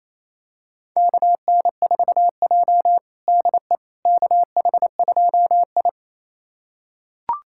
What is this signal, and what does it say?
Morse code, heard as a keyed tone, about 28 wpm, 700 Hz. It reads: KN4J DE KH2I
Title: USA Callsign DE Callsign - 4 Characters - 28wpm